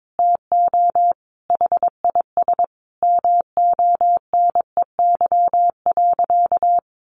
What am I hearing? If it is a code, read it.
Morse code, heard as a keyed tone, about 22 wpm, 700 Hz. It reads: TO HIS MONEY.
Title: Sentences from Top 1200 Words 22wpm